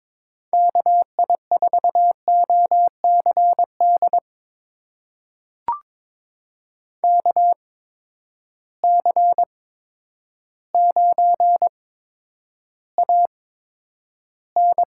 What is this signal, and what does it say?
Morse code, heard as a keyed tone, about 22 wpm, spaced as for 10 wpm, 700 Hz. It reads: KI4OCD KC9AN
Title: USA Call Signs - 6 Characters - 8x Character Spacing - 22wpm